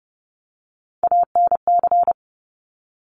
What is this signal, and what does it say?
Morse code, heard as a keyed tone, about 30 wpm, 700 Hz. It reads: ANC
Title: Top Three Letter Combinations in Words 30wpm